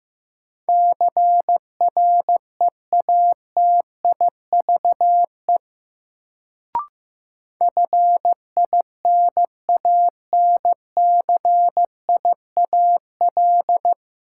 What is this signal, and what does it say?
Morse code, heard as a keyed tone, about 15 wpm, 700 Hz. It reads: CREATIVE FINANCIAL
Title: Sentences from Top 1000 Words - Review Part 5 - New Word Drill - 15wpm